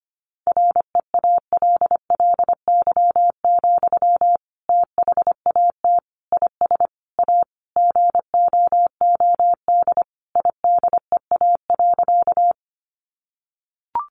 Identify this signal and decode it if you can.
Morse code, heard as a keyed tone, about 25 wpm, 700 Hz. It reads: REALLY, THAT IS A GOOD IDEA.
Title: Sentences from Top 600 Words 25wpm